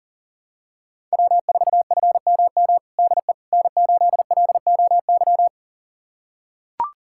Morse code, heard as a keyed tone, about 37 wpm, 700 Hz. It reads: W4FMM DE N8LOY
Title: USA Callsign DE Callsign - 5 Characters - 40wpm